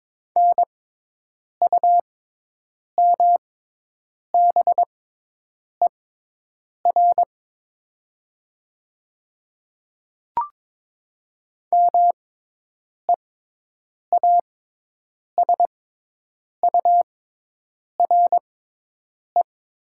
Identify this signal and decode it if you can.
Morse code, heard as a keyed tone, about 22 wpm, spaced as for 8 wpm, 700 Hz. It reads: NUMBER MEASURE
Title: 6x Character Spacing - Top 500 Words - 22wpm